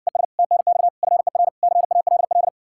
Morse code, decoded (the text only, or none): ES TNX FER CALL